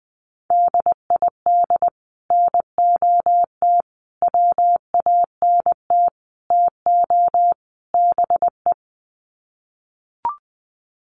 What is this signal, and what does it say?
Morse code, heard as a keyed tone, about 20 wpm, 700 Hz. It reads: DID NOT WANT TO BE
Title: Sets of 5 Words Encore 20wpm